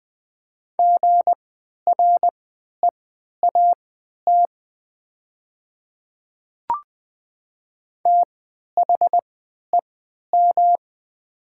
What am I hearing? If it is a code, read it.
Morse code, heard as a keyed tone, about 20 wpm, spaced as for 11 wpm, 700 Hz. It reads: GREAT THEM
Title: 3x Character Spacing - Top 200 Words - 20wpm